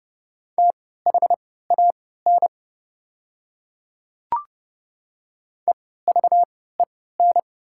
Morse code, heard as a keyed tone, about 31 wpm, spaced as for 17 wpm, 700 Hz. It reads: THAN EVEN
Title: 3x Character Spacing - Top 200 Words - 30wpm